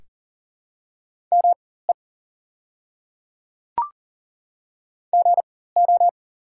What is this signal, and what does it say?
Morse code, heard as a keyed tone, about 37 wpm, spaced as for 18 wpm, 700 Hz. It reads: ME GO
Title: 4x Character Spacing - Top 300 Words - 40wpm